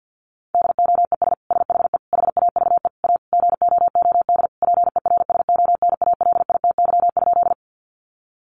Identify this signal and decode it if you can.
Morse code, heard as a keyed tone, about 50 wpm, 700 Hz. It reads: DOES SHE HAVE A GOOD PERSONALITY?
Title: Sentences from Top 1400 Words 50wpm